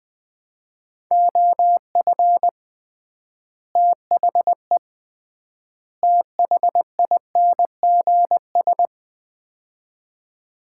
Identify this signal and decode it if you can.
Morse code, heard as a keyed tone, about 20 wpm, 700 Hz. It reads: OF THE THINGS
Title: 3x Word-Spacing - Sets of 3 Words - 20wpm